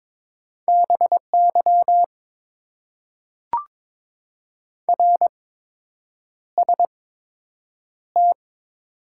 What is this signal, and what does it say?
Morse code, heard as a keyed tone, about 22 wpm, spaced as for 8 wpm, 700 Hz. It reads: BY RST
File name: Top 90 QSO Elements - 8x Character Spacing - 22wpm